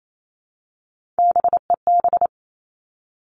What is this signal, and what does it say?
Morse code, heard as a keyed tone, about 28 wpm, 700 Hz. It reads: BEB
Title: Three Challenging Letter-Number Combinations 28wpm